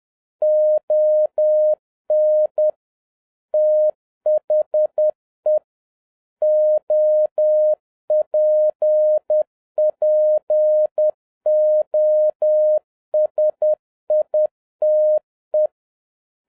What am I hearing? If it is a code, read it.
Morse code, heard as a keyed tone, about 10 wpm, 610 Hz. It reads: ON THE OPPOSITE
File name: 2011-07-mp3 / n31 THE 3.11 2 10wpm